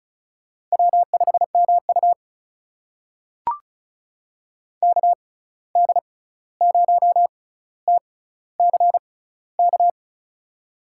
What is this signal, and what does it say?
Morse code, heard as a keyed tone, about 36 wpm, spaced as for 16 wpm, 700 Hz. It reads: W5MU KD0TCK
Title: USA Call Signs - 6x Character Spacing - 35wpm